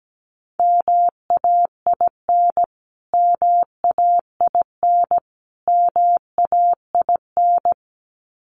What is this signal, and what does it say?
Morse code, heard as a keyed tone, about 17 wpm, 700 Hz. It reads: MAIN MAIN MAIN
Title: Top 500 Words Repeated 17wpm